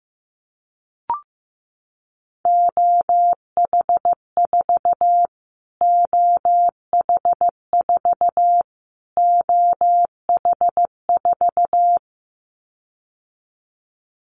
Morse code, heard as a keyed tone, about 15 wpm, 700 Hz. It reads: OH4 OH4 OH4
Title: Callsign Prefixes for Top 25 DXCC - Repeated 3x - 15wpm